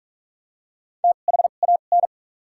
Morse code, heard as a keyed tone, about 46 wpm, spaced as for 33 wpm, 700 Hz. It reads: THAN
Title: 2x Character Spacing - Top 300 Words - 45wpm